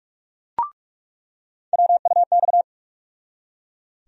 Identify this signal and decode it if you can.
Morse code, heard as a keyed tone, about 42 wpm, 700 Hz. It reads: WUX
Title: USA Call Sign Postfixes - 3 Letters 45wpm